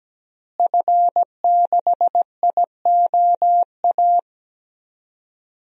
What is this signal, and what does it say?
Morse code, heard as a keyed tone, about 17 wpm, 700 Hz. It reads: F6IOA
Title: DXCC-100 Call Signs - 5 Characters - Repeated 3x - 17wpm